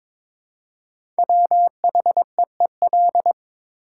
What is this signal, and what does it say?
Morse code, heard as a keyed tone, about 22 wpm, 700 Hz. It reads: WHEEL
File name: 5 Letter Words - First 3 Letters Given - 22wpm